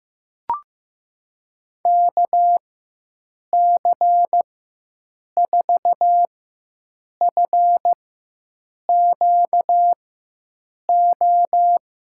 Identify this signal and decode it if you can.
Morse code, heard as a keyed tone, about 15 wpm, spaced as for 7 wpm, 700 Hz. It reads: KC4FQO
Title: USA Call Signs - 6 Characters - 4x Character Spacing - 15wpm